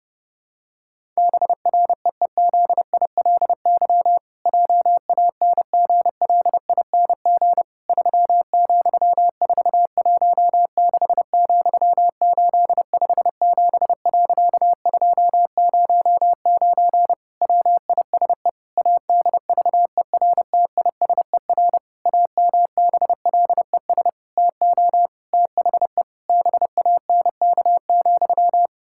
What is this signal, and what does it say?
Morse code, heard as a keyed tone, about 30 wpm, 700 Hz. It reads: BREEZILY JANGLING 3,416,857.209 WISE ADVERTISER AMBLES TO THE BANK,
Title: Pangram Challenge v2 - 30wpm